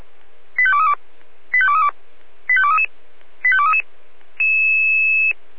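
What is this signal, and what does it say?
Five-tone selective call (ZVEI), heard as quick runs of five tones, tones of about 70 ms. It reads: 85232 85232 85233 85233
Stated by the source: ZVEI five-tone selective call received on an SDR.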